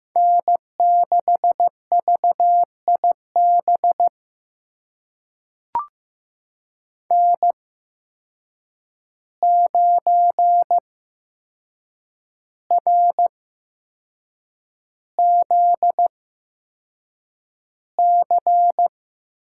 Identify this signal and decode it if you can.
Morse code, heard as a keyed tone, about 15 wpm, spaced as for 6 wpm, 700 Hz. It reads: N6VIB N9RZC
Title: USA Call Signs - 8x Character Spacing - 15wpm